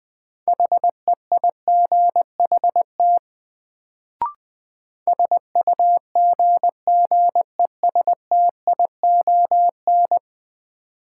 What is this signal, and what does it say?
Morse code, heard as a keyed tone, about 20 wpm, 700 Hz. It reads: HEIGHT SUGGESTION
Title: Sentences from Top 1600 Words - New Word Drill Alt - 20wpm